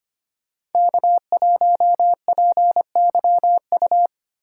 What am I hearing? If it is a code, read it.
Morse code, heard as a keyed tone, about 30 wpm, 700 Hz. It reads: K1PYU
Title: Speed Racing - USA 1x3 Call Signs 25wpm